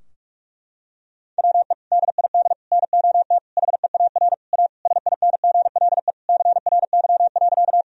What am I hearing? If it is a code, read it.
Morse code, heard as a keyed tone, about 47 wpm, 700 Hz. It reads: WE DID NOT HEAR A SINGLE CRY.
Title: Sentences from Top 1000 Words - Review Part 4 - 45wpm